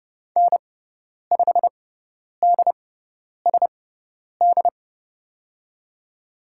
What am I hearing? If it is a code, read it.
Morse code, heard as a keyed tone, about 30 wpm, spaced as for 10 wpm, 700 Hz. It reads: N5DSD
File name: Lesson 16g - Letter c - Call Signs 30wpm